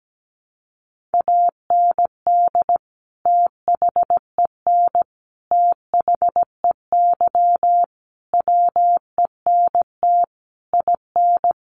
Morse code, heard as a keyed tone, about 17 wpm, 700 Hz. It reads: AND THEN THEY WENT IN
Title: Sentences from Top 1400 Words 17wpm